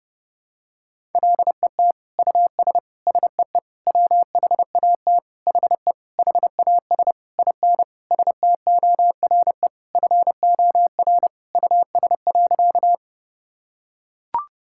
Morse code, heard as a keyed tone, about 30 wpm, 700 Hz. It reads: LET US SEE WHAT HE HAS IN STORE FOR US.
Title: Sentences from Top 1000 Words - Review Part 5 - 30wpm